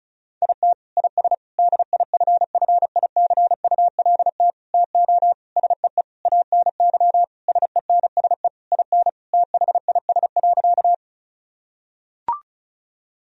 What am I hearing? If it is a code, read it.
Morse code, heard as a keyed tone, about 34 wpm, 700 Hz. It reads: IT IS DIFFICULT TO SEE ANY SENSE IN THIS.
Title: Sentences from Top 1000 Words - Review Part 4 - 35wpm